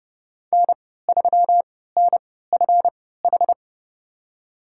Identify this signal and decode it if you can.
Morse code, heard as a keyed tone, about 29 wpm, spaced as for 17 wpm, 700 Hz. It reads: N3NFH
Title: USA Call Signs - 5 Characters - 3x Character Spacing - 30wpm